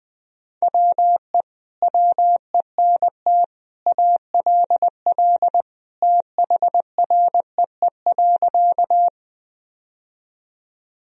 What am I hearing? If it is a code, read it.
Morse code, heard as a keyed tone, about 20 wpm, 700 Hz. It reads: WE WENT ALL THREE.
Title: Sentences from Top 1000 Words - Review Part 1 - 20wpm